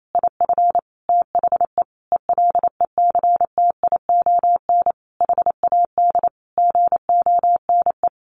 Morse code, heard as a keyed tone, about 28 wpm, 700 Hz. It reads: IF THE ELECTION HAD GONE
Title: A Sound Of Thunder 28wpm